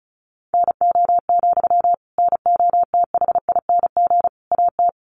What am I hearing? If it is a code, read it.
Morse code, heard as a keyed tone, about 34 wpm, 700 Hz. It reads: NO, NOTHING AT
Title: Sentences from Top 500 Words 35wpm